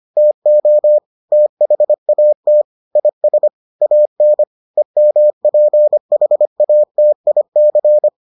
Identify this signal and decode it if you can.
Morse code, heard as a keyed tone, about 25 wpm, 600 Hz. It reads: TO THAT IS AN EMPHATIC